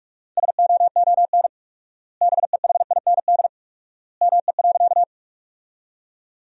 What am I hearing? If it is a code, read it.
Morse code, heard as a keyed tone, about 46 wpm, 700 Hz. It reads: SOON BEHIND ME.
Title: 4x Word-Spacing - Sentences from Top 500 Words - 45wpm